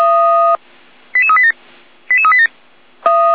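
Five-tone selective call (ZVEI), heard as quick runs of five tones, tones of about 70 ms. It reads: 88387 88387
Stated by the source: ZVEI five-tone selective call 88233 received on SDR.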